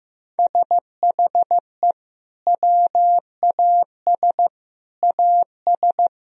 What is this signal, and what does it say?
Morse code, heard as a keyed tone, about 16 wpm, 700 Hz. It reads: SHE WAS AS
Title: Sentences from Top 1000 Words - Review Part 4 - 15wpm